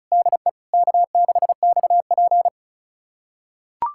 Morse code, heard as a keyed tone, about 36 wpm, 700 Hz. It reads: DE K6XP
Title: USA Callsign DE Callsign - 4 Characters - 35wpm